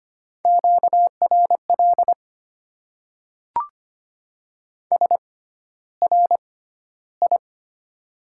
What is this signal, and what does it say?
Morse code, heard as a keyed tone, about 25 wpm, spaced as for 13 wpm, 700 Hz. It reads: QRL SRI
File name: Top 90 QSO Elements - 6x Character Spacing - 25wpm